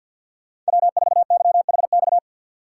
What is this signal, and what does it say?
Morse code, heard as a keyed tone, about 64 wpm, 700 Hz. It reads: W4YHX